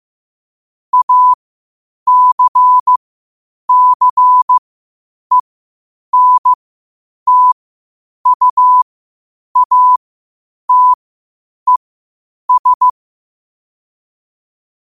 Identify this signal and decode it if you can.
Morse code, heard as a keyed tone, about 15 wpm, spaced as for 8 wpm, 1000 Hz. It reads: ACCENTUATES
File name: qotdfiles / 8wpm